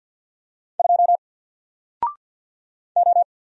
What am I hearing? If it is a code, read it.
Morse code, heard as a keyed tone, about 48 wpm, 700 Hz. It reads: 2 O
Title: Single Character (Letter:Number:Punctuation:Prosign) Alt 50wpm